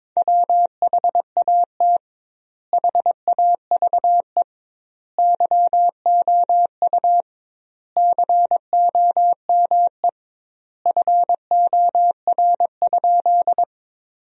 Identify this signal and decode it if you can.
Morse code, heard as a keyed tone, about 22 wpm, 700 Hz. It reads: WHAT HAVE YOU COME FOR?
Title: 2x Word-Spacing - Sentences from Top 100 Words - 22wpm